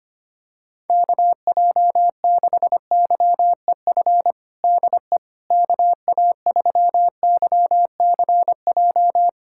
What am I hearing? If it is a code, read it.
Morse code, heard as a keyed tone, about 25 wpm, 700 Hz. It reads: KJ6YEF DE KA3YCJ
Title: USA Callsign DE Callsign 25wpm